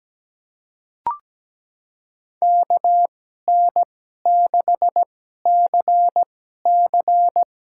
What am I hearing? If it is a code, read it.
Morse code, heard as a keyed tone, about 17 wpm, spaced as for 12 wpm, 700 Hz. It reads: KN6CC
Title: USA Call Signs - 5 Characters - 2x Character Spacing - 17wpm